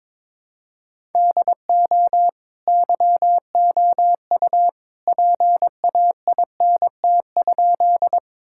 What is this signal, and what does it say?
Morse code, heard as a keyed tone, about 22 wpm, 700 Hz. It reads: DO YOU PAINT?
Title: Sentences from Top 1000 Words - Review Part 6 - 22wpm